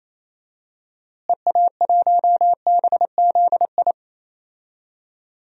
Morse code, heard as a keyed tone, about 27 wpm, 700 Hz. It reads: EA1BZI
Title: DXCC-100 Call Signs - 6 Characters - Repeated 3x - 28wpm